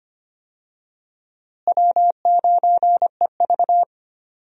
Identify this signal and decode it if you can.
Morse code, heard as a keyed tone, about 25 wpm, 700 Hz. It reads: W9EV
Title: USA Call Signs - 4 Characters - Repeated 2x - 25wpm